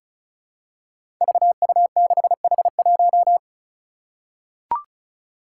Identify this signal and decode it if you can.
Morse code, heard as a keyed tone, about 36 wpm, 700 Hz. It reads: VU6H1
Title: Five Challenging Letter-Number Combinations 35wpm